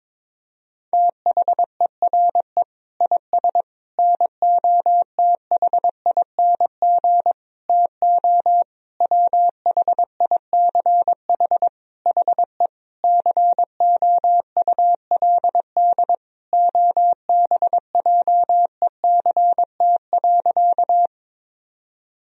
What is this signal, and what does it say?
Morse code, heard as a keyed tone, about 22 wpm, 700 Hz. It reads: THERE IS NOTHING TO WHICH HE COULD OBJECT.
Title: Sentences from Top 1000 Words - Review Part 3 - 22wpm